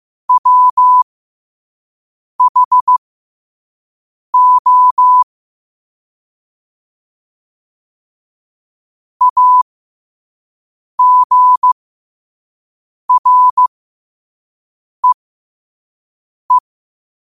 Morse code, heard as a keyed tone, about 15 wpm, spaced as for 5 wpm, 1000 Hz. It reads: WHO AGREE
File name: qotdfiles / 5wpm